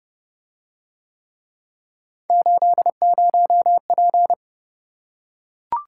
Morse code, heard as a keyed tone, about 28 wpm, 700 Hz. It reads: 80P